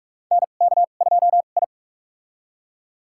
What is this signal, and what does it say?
Morse code, heard as a keyed tone, about 42 wpm, spaced as for 31 wpm, 700 Hz. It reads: NK2I
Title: USA Call Signs - 2x Character Spacing - 45wpm